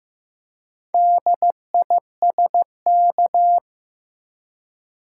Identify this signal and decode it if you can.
Morse code, heard as a keyed tone, about 15 wpm, 700 Hz. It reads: DISK
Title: Four Letter Words - 15wpm